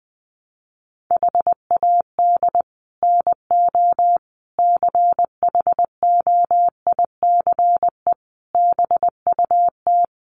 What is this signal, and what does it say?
Morse code, heard as a keyed tone, about 20 wpm, 700 Hz. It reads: HAD NO CHOICE BUT
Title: Sets of 5 Words 20wpm